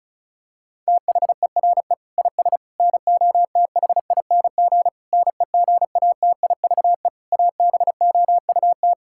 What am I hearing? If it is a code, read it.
Morse code, heard as a keyed tone, about 34 wpm, 690 Hz. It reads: THERE IS NOTHING NEGATIVE ABOUT
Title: Sentences from Top 1000 Words - Review Part 6 - 35wpm